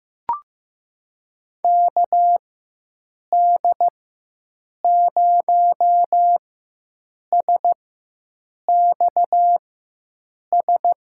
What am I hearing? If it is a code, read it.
Morse code, heard as a keyed tone, about 15 wpm, spaced as for 7 wpm, 700 Hz. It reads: KD0SXS
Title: USA Call Signs - 6 Characters - 4x Character Spacing - 15wpm